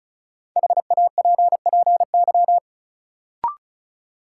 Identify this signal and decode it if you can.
Morse code, heard as a keyed tone, about 36 wpm, 700 Hz. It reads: HAPPY
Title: Sentences from Top 1000 Words - New Word Drill - 35wpm